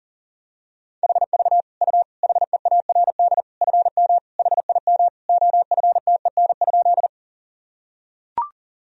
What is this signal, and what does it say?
Morse code, heard as a keyed tone, about 37 wpm, 700 Hz. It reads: HV U HEARD FM HIM OFTEN?
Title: Sentences from Top 400 Words and Abbreviations 40wpm